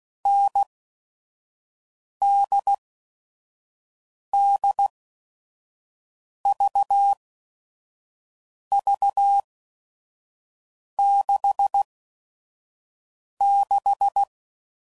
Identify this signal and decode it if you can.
Morse code, heard as a keyed tone, about 16 wpm, spaced as for 5 wpm, 800 Hz. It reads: NDDVV66
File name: Combinations / Difficult Combos / NDB NDB6